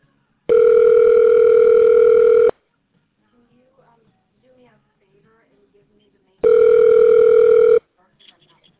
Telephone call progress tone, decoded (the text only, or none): ringback tone